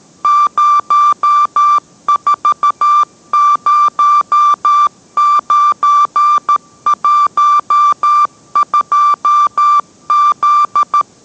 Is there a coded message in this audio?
Yes: Morse code from 0.2 to 11.0 s, heard as a keyed tone, about 13 wpm, 1180 Hz. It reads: 040912Z